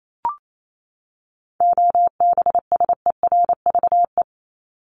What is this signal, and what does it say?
Morse code, heard as a keyed tone, about 28 wpm, 700 Hz. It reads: OBSERVE
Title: ED Suffix - Top 100 Words with Base Words 28wpm